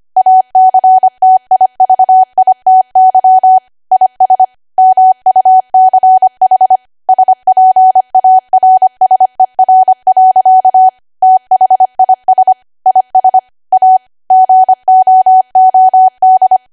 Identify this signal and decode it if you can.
Morse code, heard as a keyed tone, about 25 wpm, 750 Hz. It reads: ACTIVITY IS MUCH SPARSER. THIS IS A GOOD